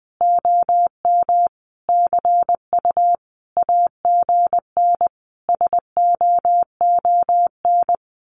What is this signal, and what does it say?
Morse code, heard as a keyed tone, about 20 wpm, 700 Hz. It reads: OM CU AGN SOON
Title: Common QSO Phrases 20wpm